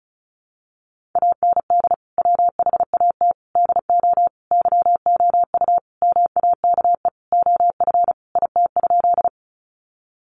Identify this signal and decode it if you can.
Morse code, heard as a keyed tone, about 34 wpm, 700 Hz. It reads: AND WHAT DO YOU MAKE OF IT?
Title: Sentences from Top 100 Words 35wpm